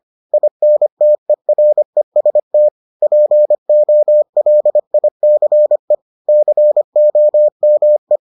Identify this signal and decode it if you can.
Morse code, heard as a keyed tone, about 25 wpm, 600 Hz. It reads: INTEREST POLICE COME